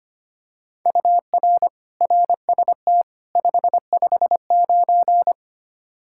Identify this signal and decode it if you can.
Morse code, heard as a keyed tone, about 25 wpm, 700 Hz. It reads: UR RST 559
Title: Practice QSO - Exchange 1 - Beginning - 25wpm